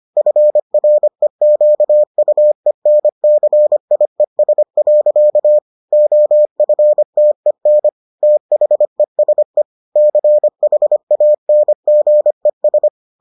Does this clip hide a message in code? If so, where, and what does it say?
Yes: Morse code from 0.2 to 12.9 s, heard as a keyed tone, about 25 wpm, 600 Hz. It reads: FREQUENCIES. OFTEN THESE CHANGES